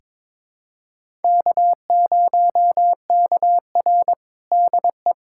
Morse code, heard as a keyed tone, about 21 wpm, 700 Hz. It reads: K0KR DE
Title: USA Callsign DE Callsign - 4 Characters - 22wpm